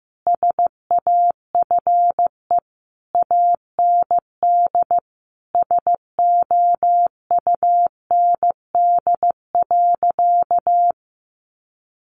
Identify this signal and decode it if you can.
Morse code, heard as a keyed tone, about 15 wpm, 700 Hz. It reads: SAFE AND SOUND.
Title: Sentences from Top 900 Words 15wpm